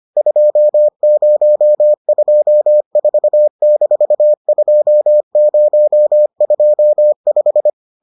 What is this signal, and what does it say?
Morse code, heard as a keyed tone, about 25 wpm, 600 Hz. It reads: 2024-2025